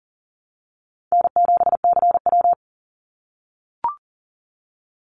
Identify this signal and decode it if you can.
Morse code, heard as a keyed tone, about 37 wpm, 700 Hz. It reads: N7CW
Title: NA SOTA Chaser Callsigns 40wpm